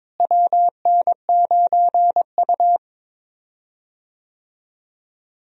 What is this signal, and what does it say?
Morse code, heard as a keyed tone, about 22 wpm, 700 Hz. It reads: WN9U
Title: USA Call Signs - 4 Characters - Repeated 2x - 22wpm